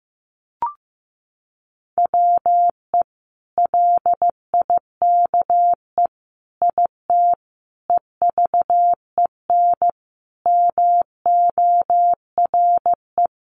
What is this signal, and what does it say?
Morse code, heard as a keyed tone, about 15 wpm, 700 Hz. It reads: WE LIKE IT EVEN MORE
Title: Sentences from Top 1100 Words 15wpm